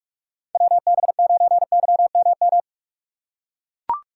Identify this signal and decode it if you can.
Morse code, heard as a keyed tone, about 44 wpm, 700 Hz. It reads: WB9YMM
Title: USA Call Signs - 6 Characters - 4x Character Spacing - 45wpm